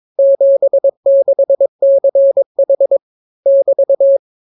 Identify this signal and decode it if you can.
Morse code, heard as a keyed tone, about 22 wpm, 550 Hz. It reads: 76CH <BT>